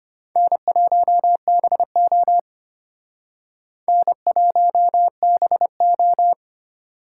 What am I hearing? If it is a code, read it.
Morse code, heard as a keyed tone, about 32 wpm, 700 Hz. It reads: N1BO N1BO